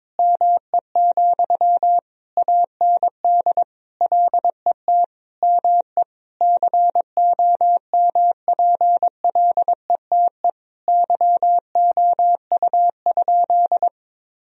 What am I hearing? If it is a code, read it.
Morse code, heard as a keyed tone, about 22 wpm, 700 Hz. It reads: ME, AND LET ME COMPLETE YOU?
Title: Sentences from Top 1000 Words - Review Part 2 - 22wpm